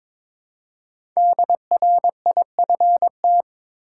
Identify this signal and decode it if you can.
Morse code, heard as a keyed tone, about 22 wpm, 700 Hz. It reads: DRIFT
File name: 5 Letter Words - First 3 Letters Given - 22wpm